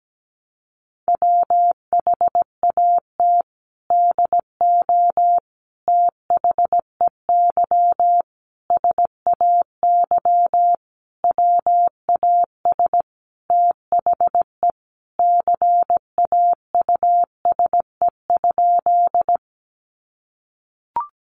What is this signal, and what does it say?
Morse code, heard as a keyed tone, about 17 wpm, 700 Hz. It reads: WHAT DO THEY SAY WAS THE CAUSE?
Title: Sentences from Top 200 Words 17wpm